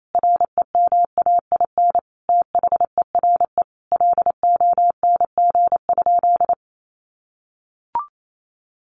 Morse code, heard as a keyed tone, about 28 wpm, 700 Hz. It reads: REMAIN THERE LONG?
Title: Sentences from Top 1200 Words 28wpm